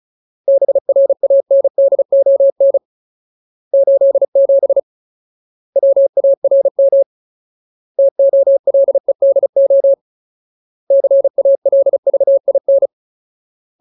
Morse code, heard as a keyed tone, about 34 wpm, 550 Hz. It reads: BRANDON 87 WARM TOLEDO CALVIN
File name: Ultimate Word List / 35WPM Ultimate